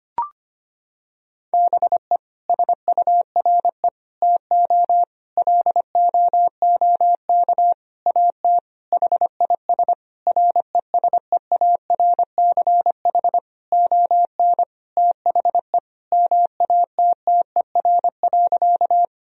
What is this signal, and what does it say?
Morse code, heard as a keyed tone, about 25 wpm, 700 Hz. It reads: BE SURE TO LOOK AT HIS RESEARCH ON THE MATTER.
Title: Sentences from Top 1000 Words - Review Part 4 - 25wpm